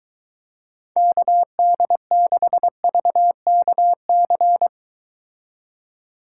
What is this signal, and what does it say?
Morse code, heard as a keyed tone, about 22 wpm, 700 Hz. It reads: KD6VKC